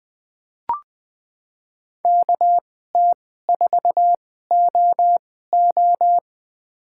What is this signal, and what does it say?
Morse code, heard as a keyed tone, about 20 wpm, spaced as for 14 wpm, 700 Hz. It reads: KT4OO
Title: USA Call Signs - 2x Character Spacing - 20wpm